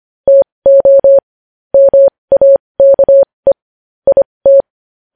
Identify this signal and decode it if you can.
Morse code, heard as a keyed tone, about 25 wpm, spaced as for 20 wpm, 550 Hz. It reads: TO MAKE IT
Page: News Headlines in Morse Code at 20 WPM